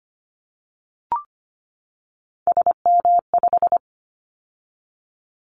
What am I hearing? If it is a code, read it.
Morse code, heard as a keyed tone, about 25 wpm, 700 Hz. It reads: SM5
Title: Callsign Prefixes for Top 25 DXCC - 25wpm